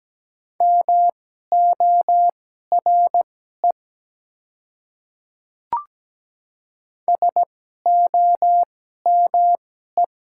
Morse code, heard as a keyed tone, about 17 wpm, spaced as for 12 wpm, 700 Hz. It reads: MORE SOME
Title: 2x Character Spacing - Top 100 Words - 17wpm